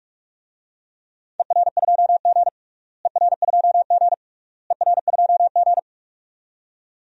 Morse code, heard as a keyed tone, about 42 wpm, 690 Hz. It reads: ER2G ER2G ER2G
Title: DXCC-100 Call Signs - 4 Characters - Repeated 3x - 45wpm